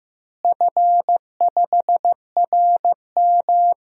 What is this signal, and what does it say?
Morse code, heard as a keyed tone, about 15 wpm, 700 Hz. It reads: F5RM
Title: DXCC-100 Call Signs - 4 Characters - Repeated 3x - 15wpm